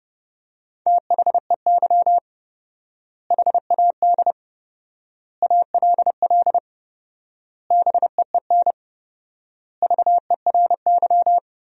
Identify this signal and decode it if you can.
Morse code, heard as a keyed tone, about 32 wpm, 700 Hz. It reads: THEY HAD ALL BEEN VERY
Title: 4x Word-Spacing - Sentences from Top 200 Words - 30wpm